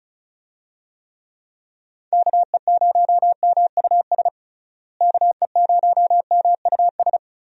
Morse code, heard as a keyed tone, about 34 wpm, 700 Hz. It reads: KE0MUS KE0MUS
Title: USA Call Signs - 6 Characters - Repeated 3x - 35wpm